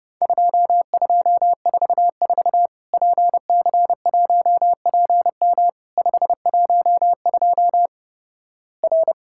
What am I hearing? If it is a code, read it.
Morse code, heard as a keyed tone, about 30 wpm, 700 Hz. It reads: 2244 PC1PM 512 R
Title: Worked All Europe DX Contest - QTC Exchange - 30wpm